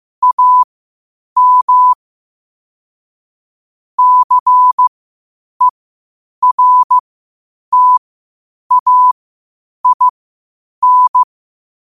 Morse code, heard as a keyed tone, about 15 wpm, spaced as for 8 wpm, 1000 Hz. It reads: AM CERTAIN
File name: qotdfiles / 8wpm